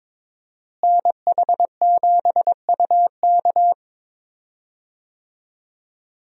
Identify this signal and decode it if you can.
Morse code, heard as a keyed tone, about 22 wpm, 700 Hz. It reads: NH7UK
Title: USA Call Signs - 5 Characters - Repeated 2x - 22wpm